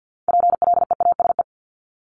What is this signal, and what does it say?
Morse code, heard as a keyed tone, about 50 wpm, 700 Hz. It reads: PLEASE
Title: Multi-letter Phonemes 50wpm